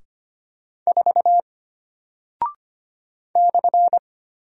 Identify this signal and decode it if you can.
Morse code, heard as a keyed tone, about 25 wpm, 700 Hz. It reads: 4 /
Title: Single Character (Letter:Number:Punctuation:Prosign) Alt 25wpm